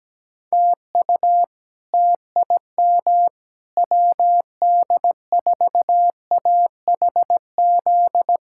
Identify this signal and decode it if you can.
Morse code, heard as a keyed tone, about 17 wpm, 700 Hz. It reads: TU TIM WD4AHZ
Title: K1USN SST QSO Practice - 17wpm